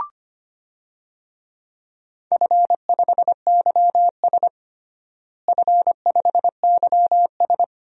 Morse code, heard as a keyed tone, about 25 wpm, 700 Hz. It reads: F5YS F5YS
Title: DXCC-100 Call Signs - 4 Characters - Repeated 2x - 25wpm